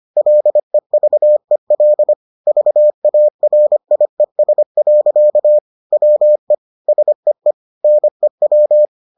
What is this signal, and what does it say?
Morse code, heard as a keyed tone, about 25 wpm, 600 Hz. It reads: LEVEL VARIES. WE SEE NEW